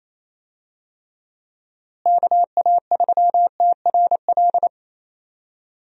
Morse code, heard as a keyed tone, about 27 wpm, 700 Hz. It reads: KA3TRL
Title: USA Call Signs - 6 Characters - Repeated 3x - 28wpm